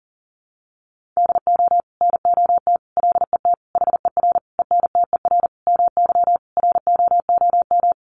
Morse code, heard as a keyed tone, about 40 wpm, 700 Hz. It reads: DO NOT LET HER ENTER MY ROOM
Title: Sentences from Top 1300 Words 40wpm